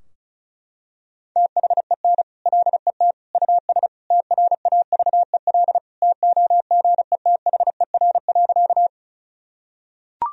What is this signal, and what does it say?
Morse code, heard as a keyed tone, about 34 wpm, 700 Hz. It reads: THEN LET US TRAVEL TOGETHER.
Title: Sentences from Top 1000 Words - Review Part 2 - 35wpm